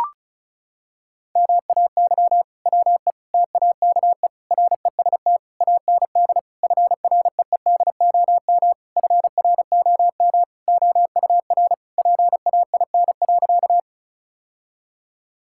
Morse code, heard as a keyed tone, about 34 wpm, 700 Hz. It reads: MAY WE TAKE REST AND FREEDOM FROM OUR PAIN.
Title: Sentences from Top 1600 Words 35wpm